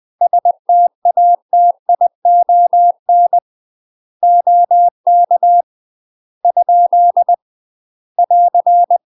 Morse code, heard as a keyed tone, about 20 wpm, 700 Hz. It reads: STATION OK ? <AR>
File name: UK MASTER QSOs TRAINING / 20WPM / UK MASTER QSOs 0071